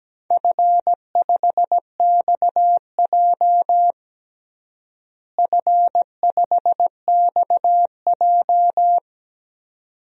Morse code, heard as a keyed tone, about 17 wpm, 700 Hz. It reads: F5XJ F5XJ
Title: DXCC-100 Call Signs - 4 Characters - Repeated 2x - 17wpm